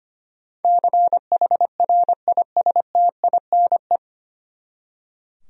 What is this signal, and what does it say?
Morse code, heard as a keyed tone, about 25 wpm, 700 Hz. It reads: CHRISTINE
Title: USA Common Names - 25wpm